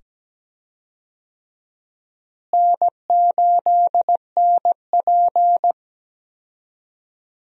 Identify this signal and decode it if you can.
Morse code, heard as a keyed tone, about 17 wpm, 700 Hz. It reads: N8NP
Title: USA Call Signs - 4 Characters - Repeated 2x - 17wpm